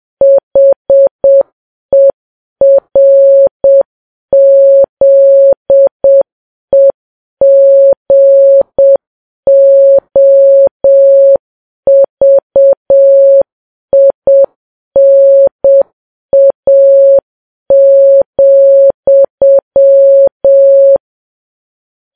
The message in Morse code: HERZEGOVINA,